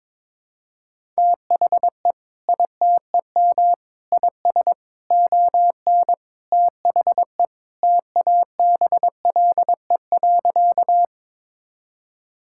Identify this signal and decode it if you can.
Morse code, heard as a keyed tone, about 21 wpm, 700 Hz. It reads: THE ITEM IS ON THE TABLE.
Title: Sentences from Top 1000 Words - Review Part 6 - 22wpm